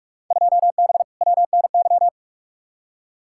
Morse code, heard as a keyed tone, about 46 wpm, 700 Hz. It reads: 2B WNY
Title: Field Day Exchanges Encore - 45wpm